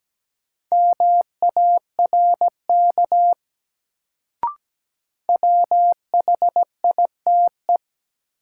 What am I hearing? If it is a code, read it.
Morse code, heard as a keyed tone, about 17 wpm, 700 Hz. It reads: MARK WHITE
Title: Sentences from Top 300 Words - New Word Drill - 17wpm